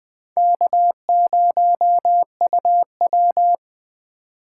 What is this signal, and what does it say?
Morse code, heard as a keyed tone, about 20 wpm, 700 Hz. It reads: K0UW
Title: Speed Racing - USA 1x2 Call Signs 17wpm